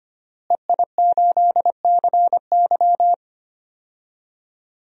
Morse code, heard as a keyed tone, about 25 wpm, 700 Hz. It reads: EI8CY
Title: DXCC-100 Call Signs - 5 Characters - Repeated 3x - 25wpm